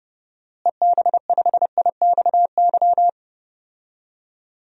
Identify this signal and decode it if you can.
Morse code, heard as a keyed tone, about 30 wpm, 700 Hz. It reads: EB5IXY
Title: DXCC-100 Call Signs - 6 Characters - Repeated 3x - 30wpm